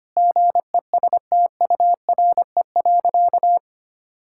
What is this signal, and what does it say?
Morse code, heard as a keyed tone, about 25 wpm, 700 Hz. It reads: GESTURE.
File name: practice / 25wpm / Difficult Sentences with Hints - 25wpm